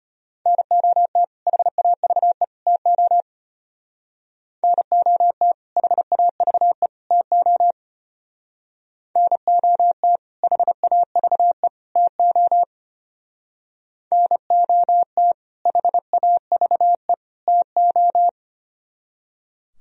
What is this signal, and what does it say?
Morse code, heard as a keyed tone, about 30 wpm, 700 Hz. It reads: NOT HAVE TO NOT HAVE TO NOT HAVE TO NOT HAVE TO
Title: Speed-Racing - Sets of 3 Words 25wpm